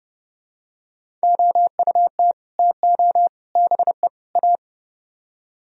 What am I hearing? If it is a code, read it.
Morse code, heard as a keyed tone, about 30 wpm, 700 Hz. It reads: OUT TO BE A